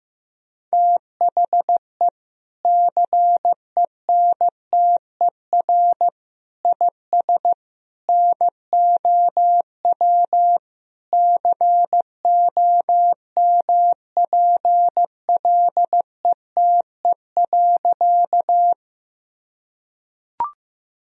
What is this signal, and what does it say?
Morse code, heard as a keyed tone, about 16 wpm, 700 Hz. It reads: THE CENTER IS NOW COMPLETE.
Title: Sentences from Top 1000 Words - Review Part 2 - 15wpm